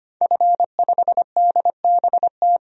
Morse code, heard as a keyed tone, about 25 wpm, 700 Hz. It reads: F5DBT
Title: DXCC-100 Call Signs - 5 Characters - 25wpm